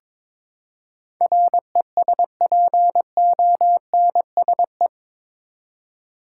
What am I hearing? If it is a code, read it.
Morse code, heard as a keyed tone, about 22 wpm, 700 Hz. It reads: RESPONSE
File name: practice / 22wpm / Eight Letter Words - 22wpm